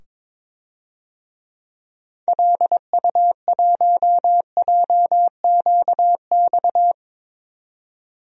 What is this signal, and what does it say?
Morse code, heard as a keyed tone, about 22 wpm, 700 Hz. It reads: LU1JQX
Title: DXCC-100 Call Signs - 6 Characters - 22wpm